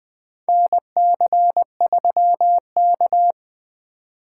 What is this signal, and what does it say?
Morse code, heard as a keyed tone, about 21 wpm, 700 Hz. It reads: NC3K
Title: USA Call Signs - 4 Characters - 3x Character Spacing - 20wpm